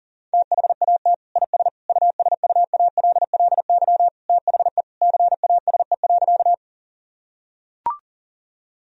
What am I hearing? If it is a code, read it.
Morse code, heard as a keyed tone, about 40 wpm, 700 Hz. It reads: THAT IS USUALLY THE CASE.
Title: Sentences from Top 1000 Words - Review Part 3 - 40wpm